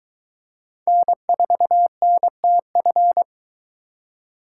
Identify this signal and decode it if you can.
Morse code, heard as a keyed tone, about 23 wpm, 700 Hz. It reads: N4NTF